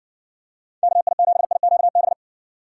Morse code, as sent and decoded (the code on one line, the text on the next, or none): -.- .. --... .. --.- -...
KI7IQB